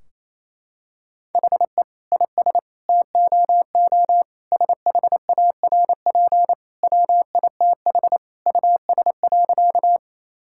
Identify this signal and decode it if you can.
Morse code, heard as a keyed tone, about 28 wpm, 700 Hz. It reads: HE IS TOO SHARP WITH US.
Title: Sentences from Top 1600 Words 28wpm